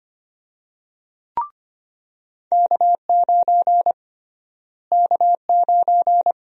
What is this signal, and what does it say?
Morse code, heard as a keyed tone, about 25 wpm, 700 Hz. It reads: K9 K9
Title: USA Call Sign Prefixes - Repeated 2x - 25wpm